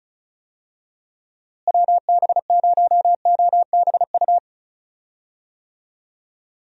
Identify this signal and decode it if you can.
Morse code, heard as a keyed tone, about 36 wpm, 700 Hz. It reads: WB0OBU
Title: USA Call Signs - 6 Characters - Repeated 3x - 35wpm